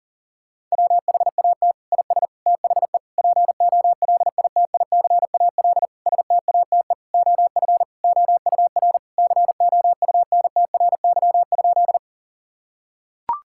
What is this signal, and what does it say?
Morse code, heard as a keyed tone, about 40 wpm, 700 Hz. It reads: WHAT IS THE POLITICAL STATE OF OUR COUNTRY?
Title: Sentences from Top 1000 Words - Review Part 5 - 40wpm